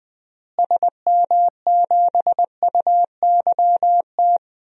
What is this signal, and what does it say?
Morse code, heard as a keyed tone, about 21 wpm, 700 Hz. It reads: SM7UYT
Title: DXCC-100 Call Signs - 6 Characters - Repeated 2x - 20wpm